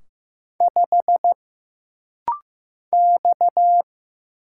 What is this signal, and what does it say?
Morse code, heard as a keyed tone, about 15 wpm, 700 Hz. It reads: 5 X
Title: Single Letter-Number Alt - Rapid-Fire 15wpm